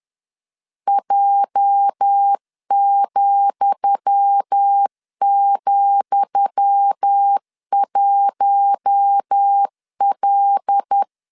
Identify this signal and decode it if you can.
Morse code, heard as a keyed tone, about 10 wpm, 800 Hz. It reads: J,,1L
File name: LEZIONI / 10 SIGNS / 10D COMA 10wpm 3013